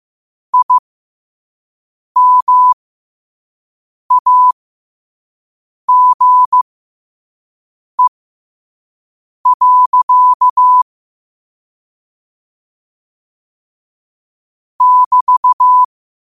Morse code, heard as a keyed tone, about 15 wpm, spaced as for 5 wpm, 1000 Hz. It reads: IMAGE. <BT>
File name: qotdfiles / 5wpm